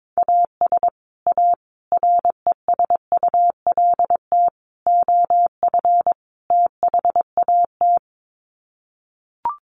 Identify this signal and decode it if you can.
Morse code, heard as a keyed tone, about 22 wpm, 700 Hz. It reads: AS A RESULT OF THAT
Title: Sets of 5 Words Encore 22wpm